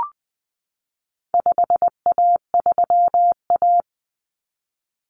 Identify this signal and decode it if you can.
Morse code, heard as a keyed tone, about 20 wpm, 700 Hz. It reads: 5A3A
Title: Common Cut Numbers - 4-Digit 20wpm